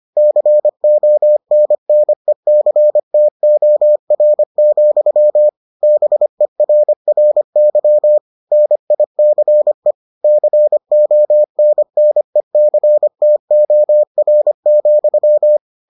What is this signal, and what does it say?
Morse code, heard as a keyed tone, about 25 wpm, 600 Hz. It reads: CONNECTOR, BERRY NICE CONNECTOR,